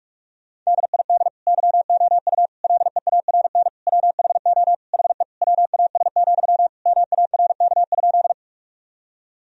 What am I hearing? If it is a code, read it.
Morse code, heard as a keyed tone, about 46 wpm, 700 Hz. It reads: DID YOU LEARN WHO HE WAS, MARK?
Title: Sentences from Top 1000 Words - Review Part 1 - 45wpm